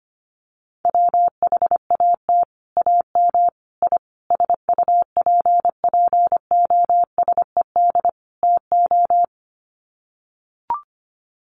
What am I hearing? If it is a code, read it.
Morse code, heard as a keyed tone, about 25 wpm, 700 Hz. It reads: WHAT AM I SUPPOSED TO
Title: Sets of 5 Words Encore2 25wpm